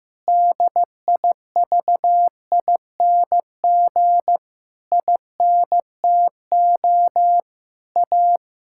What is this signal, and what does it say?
Morse code, heard as a keyed tone, about 16 wpm, 700 Hz. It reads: DIVING INTO A
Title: Pangram Challenge v2 - 15wpm